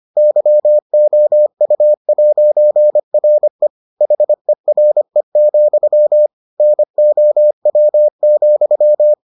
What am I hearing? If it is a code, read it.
Morse code, heard as a keyed tone, about 25 wpm, 600 Hz. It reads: YOU'RE HERE, NOW,